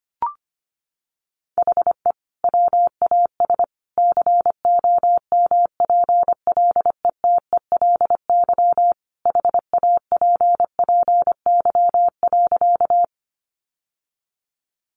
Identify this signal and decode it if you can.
Morse code, heard as a keyed tone, about 25 wpm, 700 Hz. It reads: HE WAS COMPLETELY HAPPY.
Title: Sentences from Top 1100 Words 25wpm